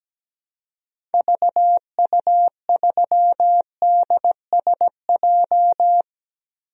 Morse code, heard as a keyed tone, about 17 wpm, 700 Hz. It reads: VU3DSJ
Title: DXCC-100 Call Signs - 6 Characters - Repeated 3x - 17wpm